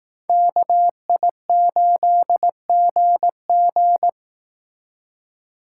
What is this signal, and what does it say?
Morse code, heard as a keyed tone, about 18 wpm, 700 Hz. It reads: KI8GG
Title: Speed Racing - USA 2x2 Call Signs 15wpm